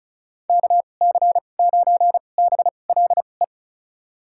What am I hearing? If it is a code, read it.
Morse code, heard as a keyed tone, about 34 wpm, spaced as for 25 wpm, 700 Hz. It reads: KC9BLE
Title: USA Call Signs - 6 Characters - 2x Character Spacing - 35wpm